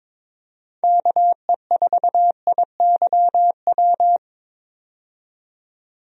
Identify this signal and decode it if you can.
Morse code, heard as a keyed tone, about 21 wpm, 700 Hz. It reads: KE4IYW
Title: USA Call Signs - 6 Characters - 22wpm